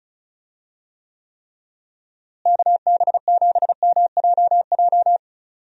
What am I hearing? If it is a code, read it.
Morse code, heard as a keyed tone, about 34 wpm, 700 Hz. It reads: KB7MJJ
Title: USA Call Signs - 6 Characters - Repeated 3x - 35wpm